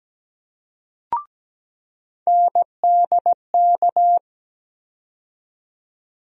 Morse code, heard as a keyed tone, about 17 wpm, 700 Hz. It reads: NDK
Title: USA Call Sign Postfixes - 3 Letters 17wpm